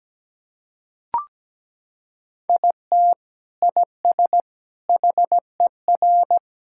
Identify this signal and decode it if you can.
Morse code, heard as a keyed tone, about 17 wpm, 700 Hz. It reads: IT IS HER
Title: Sentences from Top 1600 Words 17wpm